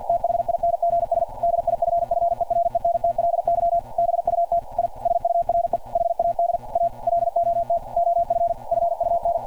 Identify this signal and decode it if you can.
Morse code, heard as a keyed tone, about 52 wpm, 690 Hz. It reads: THINGS DIFFERENTLY BUT I SURE HAD A LOT OF FUN
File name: archived / CRSnet